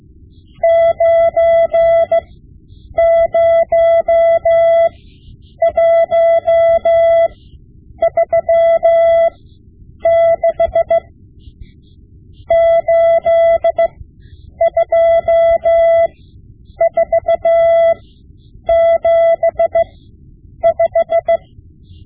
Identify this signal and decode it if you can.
Morse code, heard as a keyed tone, about 16 wpm, spaced as for 8 wpm, 670 Hz. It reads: 90136 82475
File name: Morse Lessons / Morse Lessons / Tape-5B